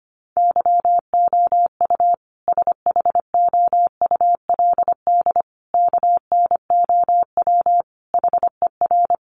Morse code, heard as a keyed tone, about 25 wpm, 700 Hz. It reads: YOU SHOULD KNOW HER
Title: Sentences from Top 400 Words 25wpm